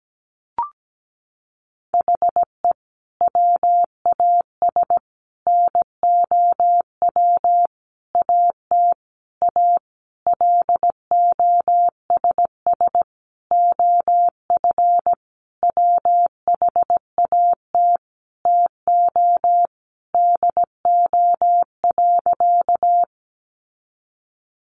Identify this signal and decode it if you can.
Morse code, heard as a keyed tone, about 17 wpm, 700 Hz. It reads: HE WAS NOW AT A LOSS OF WHAT TO DO.
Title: Sentences from Top 1400 Words 17wpm